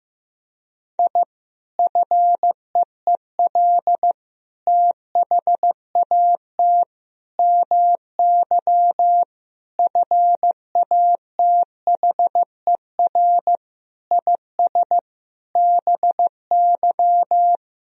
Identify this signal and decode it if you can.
Morse code, heard as a keyed tone, about 15 wpm, 700 Hz. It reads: I FEEL THAT MY FATHER IS BY
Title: Sentences from Top 1000 Words - Review Part 1 - 15wpm